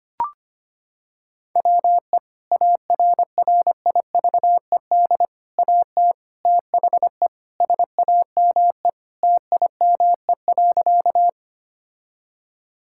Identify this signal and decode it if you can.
Morse code, heard as a keyed tone, about 25 wpm, 700 Hz. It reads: WE ARRIVED AT THE SAME TIME.
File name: practice / 25wpm / Sentences with 3-Word Phrases - 25wpm